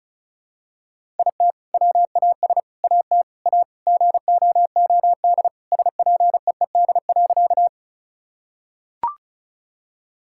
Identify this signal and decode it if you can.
Morse code, heard as a keyed tone, about 36 wpm, 700 Hz. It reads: IT WAS AT A GOOD SPEED.
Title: Sentences from Top 1000 Words - Review Part 5 - 35wpm